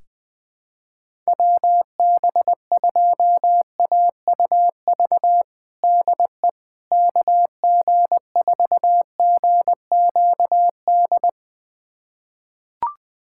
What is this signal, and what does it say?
Morse code, heard as a keyed tone, about 20 wpm, 700 Hz. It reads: WB2AUV DE KG4GQD
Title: USA Callsign DE Callsign - 6 Characters - 20wpm